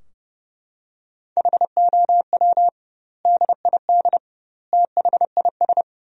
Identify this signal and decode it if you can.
Morse code, heard as a keyed tone, about 30 wpm, 700 Hz. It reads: HOW DID THIS
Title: 2x Word-Spacing - Sentences from Top 400 Words - 30wpm